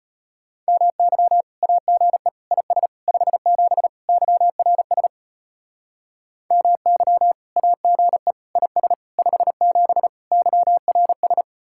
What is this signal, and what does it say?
Morse code, heard as a keyed tone, about 33 wpm, 700 Hz. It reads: MY AGE IS 57 YRS MY AGE IS 57 YRS